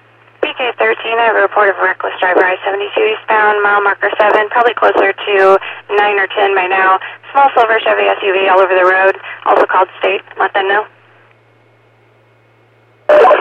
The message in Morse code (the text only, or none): none